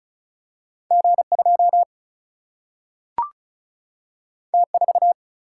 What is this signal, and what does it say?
Morse code, heard as a keyed tone, about 36 wpm, 700 Hz. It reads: G2 T4
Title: Single-Letter Single-Number 35wpm